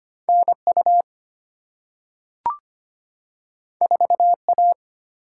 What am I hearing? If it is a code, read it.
Morse code, heard as a keyed tone, about 25 wpm, 690 Hz. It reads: NU 4A
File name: Two Letter-Number Combinations - 25wpm